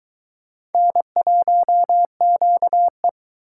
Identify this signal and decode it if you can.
Morse code, heard as a keyed tone, about 22 wpm, 700 Hz. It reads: N1QE